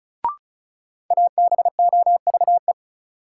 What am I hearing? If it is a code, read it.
Morse code, heard as a keyed tone, about 36 wpm, 700 Hz. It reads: ABOVE
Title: Sentences from Top 1000 Words - Review Part 2 - New Word Drill - 35wpm